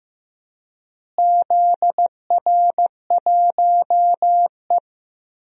Morse code, heard as a keyed tone, about 15 wpm, 700 Hz. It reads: ZR1E
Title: DXCC-100 Call Signs - 4 Characters - Repeated 2x - 15wpm